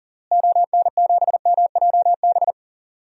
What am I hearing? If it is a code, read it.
Morse code, heard as a keyed tone, about 40 wpm, 700 Hz. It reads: ON7MJB
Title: DXCC-100 Call Signs - 6 Characters - Repeated 3x - 40wpm